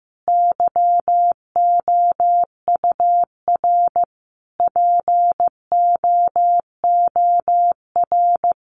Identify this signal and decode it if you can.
Morse code, heard as a keyed tone, about 15 wpm, 700 Hz. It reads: YOUR POOR
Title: Sentences from Top 1000 Words 15wpm